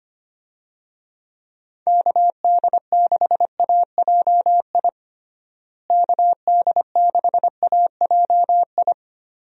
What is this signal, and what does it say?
Morse code, heard as a keyed tone, about 25 wpm, 700 Hz. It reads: KD6AJI KD6AJI
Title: USA Call Signs - 6 Characters - Repeated 2x - 25wpm